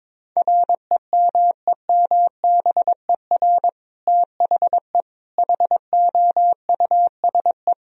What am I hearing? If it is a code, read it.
Morse code, heard as a keyed tone, about 22 wpm, 700 Hz. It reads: REMEMBER THE HOUSE
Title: Sentences from Top 1000 Words - Review Part 1 - 22wpm